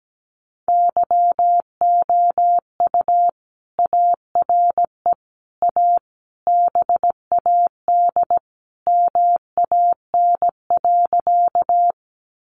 Morse code, heard as a keyed tone, about 17 wpm, 700 Hz. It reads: YOU ARE A BAD MAN.
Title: Sentences from Top 700 Words 17wpm